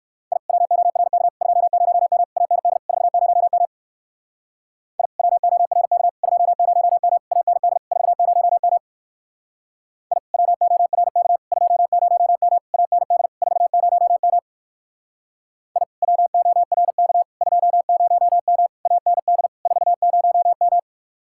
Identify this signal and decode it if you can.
Morse code, heard as a keyed tone, about 60 wpm, 700 Hz. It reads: I WORK 20M AND 40M I WORK 20M AND 40M I WORK 20M AND 40M I WORK 20M AND 40M